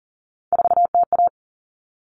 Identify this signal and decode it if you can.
Morse code, heard as a keyed tone, about 40 wpm, 700 Hz. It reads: 4TA
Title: Common Cut Numbers - 3-Digit 40wpm